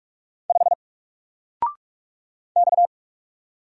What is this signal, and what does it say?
Morse code, heard as a keyed tone, about 46 wpm, 700 Hz. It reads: 5 X
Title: Single Letter-Number 45wpm